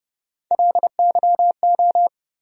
Morse code, heard as a keyed tone, about 30 wpm, 700 Hz. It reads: LYO
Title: USA Call Sign Postfixes - 3 Letters 30wpm